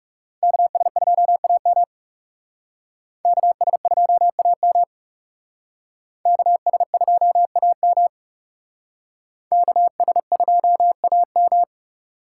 Morse code, heard as a keyed tone, about 38 wpm, 700 Hz. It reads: KS2AM KS2AM KS2AM KS2AM